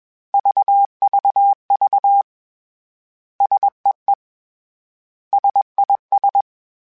Morse code, heard as a keyed tone, about 21 wpm, 800 Hz. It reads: VVV SEE SIS
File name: S vs H / S vs H mp3s / S vs H @20